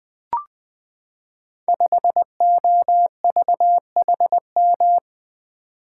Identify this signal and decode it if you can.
Morse code, heard as a keyed tone, about 20 wpm, 700 Hz. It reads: 5OVHM
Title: Code Groups - 5 Letter:Numbers - 20wpm